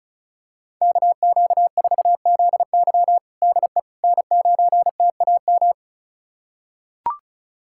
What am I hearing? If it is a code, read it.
Morse code, heard as a keyed tone, about 36 wpm, 690 Hz. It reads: KQ4ZY DE N9TAM
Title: USA Callsign DE Callsign - 5 Characters - 35wpm